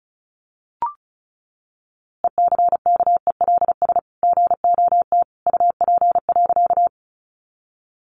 Morse code, heard as a keyed tone, about 34 wpm, 700 Hz. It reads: ECKELS GOT UP.
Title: A Sound Of Thunder 35wpm